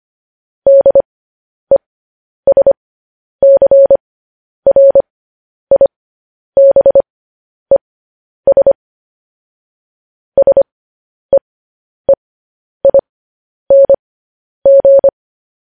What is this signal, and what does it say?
Morse code, heard as a keyed tone, about 25 wpm, spaced as for 10 wpm, 550 Hz. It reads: DESCRIBES SEEING